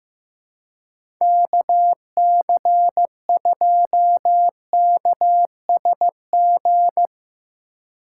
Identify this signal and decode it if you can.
Morse code, heard as a keyed tone, about 15 wpm, 700 Hz. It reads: KC2KSG
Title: USA Call Signs - 6 Characters - 4x Character Spacing - 15wpm